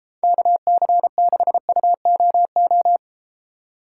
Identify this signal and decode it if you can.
Morse code, heard as a keyed tone, about 34 wpm, 700 Hz. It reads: KC6UOO